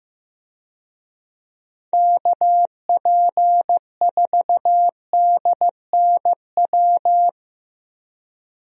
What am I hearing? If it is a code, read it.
Morse code, heard as a keyed tone, about 15 wpm, 700 Hz. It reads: KP4DNW
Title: USA Call Signs - 6 Characters - Repeated 3x - 15wpm